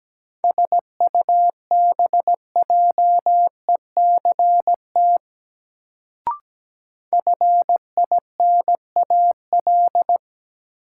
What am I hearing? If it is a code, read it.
Morse code, heard as a keyed tone, about 17 wpm, 700 Hz. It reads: SUBJECT FINAL
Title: Sentences from Top 800 Words - New Word Drill - 17wpm